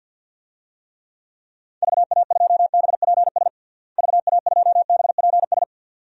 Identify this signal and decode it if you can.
Morse code, heard as a keyed tone, about 47 wpm, 700 Hz. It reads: VA2BPS VA2BPS
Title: DXCC-100 Call Signs - 6 Characters - Repeated 2x - 50wpm